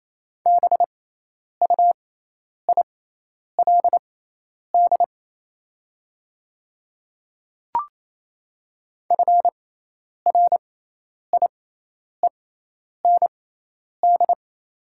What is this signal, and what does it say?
Morse code, heard as a keyed tone, about 27 wpm, spaced as for 10 wpm, 690 Hz. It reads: BUILD FRIEND
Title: 6x Character Spacing - Top 300 Words - 28wpm